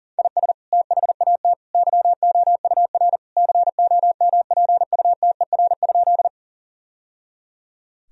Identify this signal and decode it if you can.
Morse code, heard as a keyed tone, about 40 wpm, 700 Hz. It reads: IS THAT YOUR COMPUTER?
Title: Sentences from Top 1000 Words - Review Part 3 - 40wpm